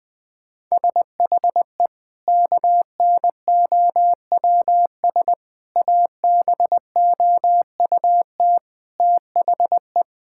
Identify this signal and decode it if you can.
Morse code, heard as a keyed tone, about 20 wpm, 700 Hz. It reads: SHE KNOWS ABOUT THE
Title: Sentences with 2-Word Phrases - 20wpm